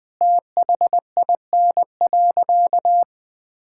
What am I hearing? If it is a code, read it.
Morse code, heard as a keyed tone, about 21 wpm, 700 Hz. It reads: THIN.
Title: Sentences from Top 1600 Words 20wpm